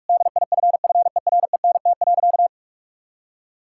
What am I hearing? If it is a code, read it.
Morse code, heard as a keyed tone, about 46 wpm, 700 Hz. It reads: DIFFERENT.
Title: Sentences from Top 1600 Words 45wpm